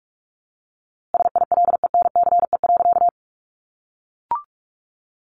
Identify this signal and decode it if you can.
Morse code, heard as a keyed tone, about 44 wpm, 700 Hz. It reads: SILENCE.
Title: A Sound Of Thunder 45wpm